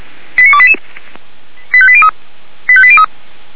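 Five-tone selective call (ZVEI), heard as quick runs of five tones, tones of about 70 ms. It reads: 89288 85803 85803
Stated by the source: ZVEI five-tone call 89288 received by SDR, on the Feuerwehr Orlamuende audio server.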